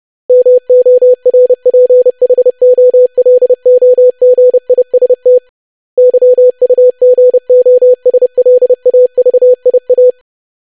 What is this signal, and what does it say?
Morse code, heard as a keyed tone, about 30 wpm, 500 Hz. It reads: MORPHOLOGIST YUGOSLAVIA
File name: lesson / web / longwords-25wpm